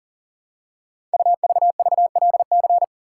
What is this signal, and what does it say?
Morse code, heard as a keyed tone, about 42 wpm, 700 Hz. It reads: UVVLC
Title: Code Groups - 5 Letters - 40wpm